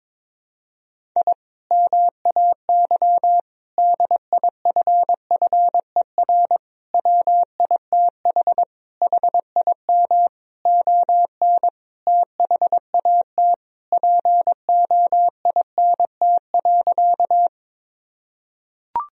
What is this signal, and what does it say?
Morse code, heard as a keyed tone, about 22 wpm, 700 Hz. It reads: I MAY DIFFER WITH HIM ON THAT POINT.
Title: Sentences from Top 1000 Words - Review Part 1 - 22wpm